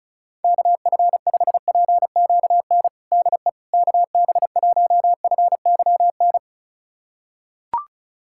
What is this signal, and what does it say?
Morse code, heard as a keyed tone, about 34 wpm, 700 Hz. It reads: KF5PQN DE KB1FYN
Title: USA Callsign DE Callsign - 6 Characters - 35wpm